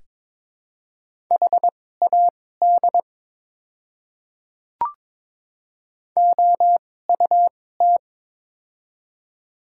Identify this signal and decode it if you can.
Morse code, heard as a keyed tone, about 22 wpm, spaced as for 16 wpm, 700 Hz. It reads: HAD OUT
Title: 2x Character Spacing - Top 200 Words - 22wpm